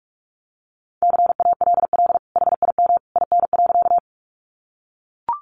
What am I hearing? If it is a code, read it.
Morse code, heard as a keyed tone, about 44 wpm, 700 Hz. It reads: CALL HIM IN.
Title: Sentences from Top 100 Words 45wpm